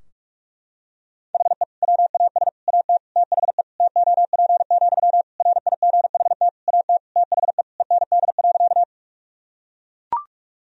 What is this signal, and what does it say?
Morse code, heard as a keyed tone, about 44 wpm, 700 Hz. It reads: HE WAS AT THE TOP, RIGHT AT THE END.
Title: Sentences from Top 1000 Words - Review Part 2 - 45wpm